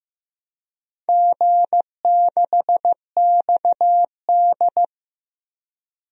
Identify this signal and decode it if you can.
Morse code, heard as a keyed tone, about 15 wpm, 700 Hz. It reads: G6XD
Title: DXCC-100 Call Signs - 4 Characters - 15wpm